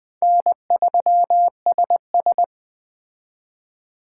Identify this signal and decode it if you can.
Morse code, heard as a keyed tone, about 20 wpm, 700 Hz. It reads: N3SS
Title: Speed Racing - USA 1x2 Call Signs 17wpm